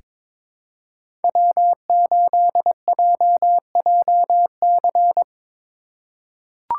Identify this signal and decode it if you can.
Morse code, heard as a keyed tone, about 22 wpm, 700 Hz. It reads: W8JJC
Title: USA Call Signs - 5 Characters - 4x Character Spacing - 22wpm